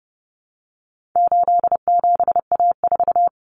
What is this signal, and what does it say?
Morse code, heard as a keyed tone, about 30 wpm, 700 Hz. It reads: 87A4
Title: Common Cut Numbers - 4-Digit 30wpm